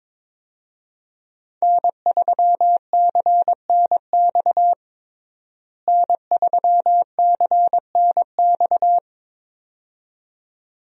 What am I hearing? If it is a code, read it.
Morse code, heard as a keyed tone, about 22 wpm, 700 Hz. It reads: N3CNX N3CNX
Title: USA Call Signs - 5 Characters - Repeated 2x - 22wpm